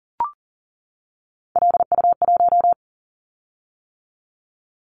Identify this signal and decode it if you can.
Morse code, heard as a keyed tone, about 41 wpm, 700 Hz. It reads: LU1
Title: Callsign Prefixes for Top 100 DXCC - 40wpm